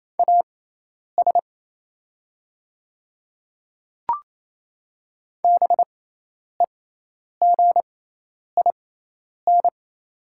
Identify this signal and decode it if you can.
Morse code, heard as a keyed tone, about 28 wpm, spaced as for 10 wpm, 700 Hz. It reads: AS BEGIN
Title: 6x Character Spacing - Top 500 Words - 28wpm